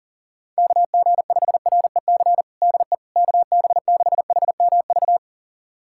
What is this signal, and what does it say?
Morse code, heard as a keyed tone, about 40 wpm, 700 Hz. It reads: KG5REC DE KB6HMV
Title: USA Callsign DE Callsign - 6 Characters - 40wpm